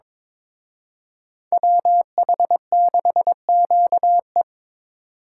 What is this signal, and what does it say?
Morse code, heard as a keyed tone, about 26 wpm, 700 Hz. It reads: WH6QE
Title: Speed Racing - USA 2x2 Call Signs 22wpm